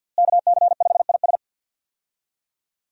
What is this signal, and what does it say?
Morse code, heard as a keyed tone, about 52 wpm, 700 Hz. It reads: KC5IS
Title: USA Call Signs - 5 Characters - Repeated 3x - 50wpm